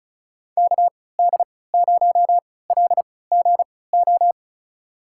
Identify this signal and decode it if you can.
Morse code, heard as a keyed tone, about 36 wpm, spaced as for 20 wpm, 700 Hz. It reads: KD0LGO
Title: USA Call Signs - 6 Characters - 3x Character Spacing - 35wpm